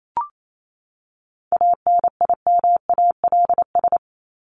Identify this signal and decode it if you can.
Morse code, heard as a keyed tone, about 28 wpm, 700 Hz. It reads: ANIMALS
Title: ES:S Suffix - Top 100 Words with Base Words 28wpm